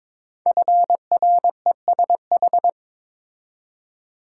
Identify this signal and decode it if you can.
Morse code, heard as a keyed tone, about 22 wpm, 700 Hz. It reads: FRESH
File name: 5 Letter Words - First 3 Letters Given - 22wpm